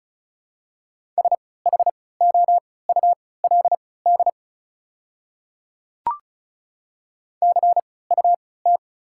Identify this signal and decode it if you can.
Morse code, heard as a keyed tone, about 36 wpm, spaced as for 20 wpm, 690 Hz. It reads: SHOULD CUT
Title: 3x Character Spacing - Top 500 Words - 35wpm